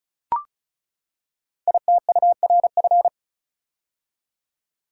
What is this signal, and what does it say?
Morse code, heard as a keyed tone, about 36 wpm, 690 Hz. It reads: ITURF
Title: Code Groups - 5 Letters - 35wpm